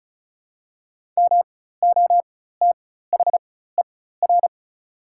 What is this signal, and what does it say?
Morse code, heard as a keyed tone, about 34 wpm, spaced as for 16 wpm, 700 Hz. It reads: MOTHER
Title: 4x Character Spacing - Top 200 Words - 35wpm